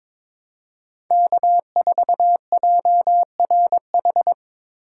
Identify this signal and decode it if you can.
Morse code, heard as a keyed tone, about 22 wpm, 700 Hz. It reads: K4JRH
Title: USA Call Signs - 5 Characters - 4x Character Spacing - 22wpm